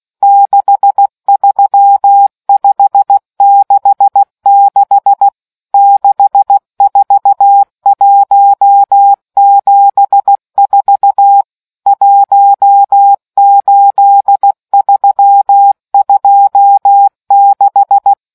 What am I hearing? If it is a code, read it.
Morse code, heard as a keyed tone, about 16 wpm, 800 Hz. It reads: 63566 64174 18326